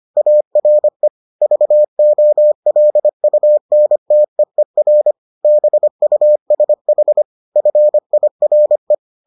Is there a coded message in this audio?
Yes: Morse code from 0.2 to 8.9 s, heard as a keyed tone, about 25 wpm, 600 Hz. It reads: ARE VOLUNTEER BUSH FIRE